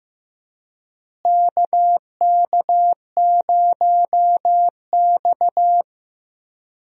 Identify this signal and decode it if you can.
Morse code, heard as a keyed tone, about 15 wpm, 700 Hz. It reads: KK0X
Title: USA Call Signs - 4 Characters - Repeated 3x - 15wpm